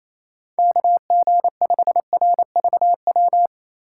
Morse code, heard as a keyed tone, about 28 wpm, 700 Hz. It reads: KG5RVW
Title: USA Call Signs - 6 Characters - 2x Character Spacing - 28wpm